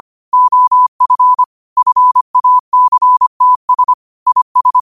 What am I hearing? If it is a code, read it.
Morse code, heard as a keyed tone, about 25 wpm, 1000 Hz. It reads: OF FACTS IS
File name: qotdfiles / 25wpm